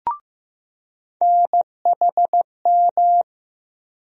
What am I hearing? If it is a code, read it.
Morse code, heard as a keyed tone, about 15 wpm, 700 Hz. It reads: NHM
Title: USA Call Sign Postfixes - 3 Letters 15wpm